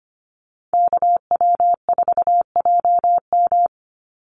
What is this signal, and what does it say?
Morse code, heard as a keyed tone, about 25 wpm, 700 Hz. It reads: KW4JM
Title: NA SOTA Chaser Callsigns 25wpm